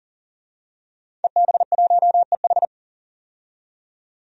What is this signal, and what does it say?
Morse code, heard as a keyed tone, about 40 wpm, 700 Hz. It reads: EB1EH
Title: DXCC-100 Call Signs - 5 Characters - 40wpm